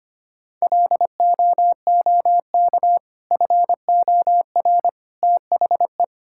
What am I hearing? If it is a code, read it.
Morse code, heard as a keyed tone, about 25 wpm, 690 Hz. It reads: LOOK FOR THE
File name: practice / 25wpm / Sentences with 2-Word Phrases - 25wpm